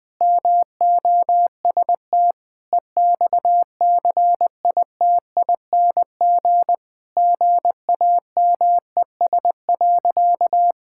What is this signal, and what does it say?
Morse code, heard as a keyed tone, about 20 wpm, 700 Hz. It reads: MOST EXCITING GAMES.
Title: Sentences with 5-Word Phrases - 20wpm